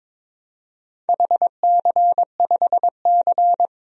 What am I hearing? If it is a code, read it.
Morse code, heard as a keyed tone, about 22 wpm, 700 Hz. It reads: HC5C
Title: DXCC-100 Call Signs - 4 Characters - 22wpm